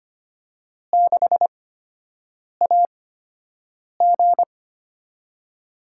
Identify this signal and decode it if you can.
Morse code, heard as a keyed tone, about 25 wpm, spaced as for 7 wpm, 700 Hz. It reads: 6AG
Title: USA Call Signs - 4 Characters - 8x Character Spacing - 25wpm